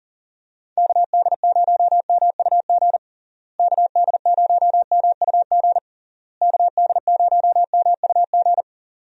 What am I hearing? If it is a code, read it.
Morse code, heard as a keyed tone, about 39 wpm, 700 Hz. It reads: KD0MUG KD0MUG KD0MUG
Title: USA Call Signs - 6 Characters - Repeated 3x - 40wpm